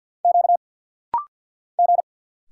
Morse code, heard as a keyed tone, about 50 wpm, 700 Hz. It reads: Q G
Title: Single Letters - Rapid-Fire 50wpm